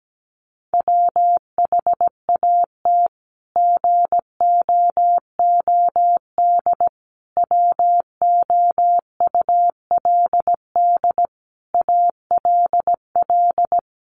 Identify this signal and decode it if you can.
Morse code, heard as a keyed tone, about 17 wpm, 700 Hz. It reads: WHAT GOOD WOULD ALL
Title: Sentences from Top 1200 Words 17wpm